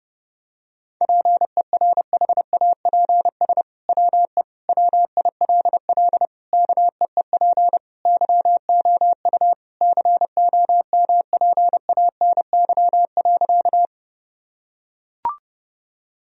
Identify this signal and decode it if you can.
Morse code, heard as a keyed tone, about 32 wpm, 700 Hz. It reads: PERHAPS WE WILL KEEP YOU COMPANY.
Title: Sentences from Top 1000 Words - Review Part 3 - 30wpm